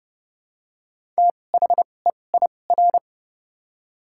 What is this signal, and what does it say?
Morse code, heard as a keyed tone, about 31 wpm, spaced as for 22 wpm, 700 Hz. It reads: THEIR
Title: 2x Character Spacing - Top 100 Words - 30wpm